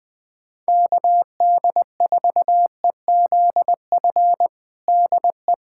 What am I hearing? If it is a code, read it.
Morse code, heard as a keyed tone, about 21 wpm, 700 Hz. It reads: KD4EZF DE
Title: USA Callsign DE Callsign 20wpm